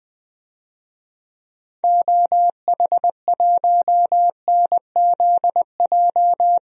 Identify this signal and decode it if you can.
Morse code, heard as a keyed tone, about 20 wpm, 700 Hz. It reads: OH1NZJ
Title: DXCC-100 Call Signs - 6 Characters - 20wpm